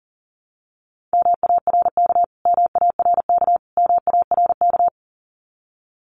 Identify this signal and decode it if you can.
Morse code, heard as a keyed tone, about 40 wpm, 700 Hz. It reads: MARK MARK MARK
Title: Top 300 Words Repeated 40wpm